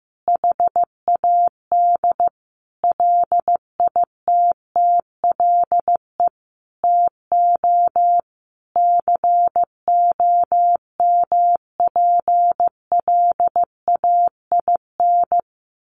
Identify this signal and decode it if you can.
Morse code, heard as a keyed tone, about 15 wpm, 700 Hz. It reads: HAD LITTLE TO COMPLAIN
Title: Sentences from Top 1300 Words 15wpm